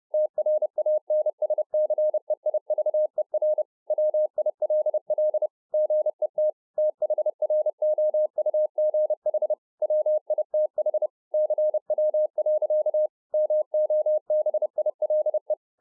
Morse code, heard as a keyed tone, about 30 wpm, 610 Hz. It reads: TRANSCEIVER WILL GET THROUGH WITH CW. MOBILE